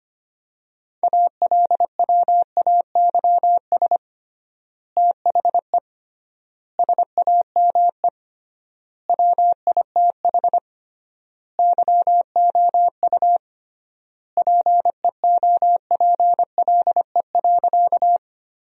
Morse code, heard as a keyed tone, about 25 wpm, 700 Hz. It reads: ALWAYS THE SAME WITH YOU PEOPLE.
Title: 3x Word-Spacing - Sentences from Top 300 Words - 25wpm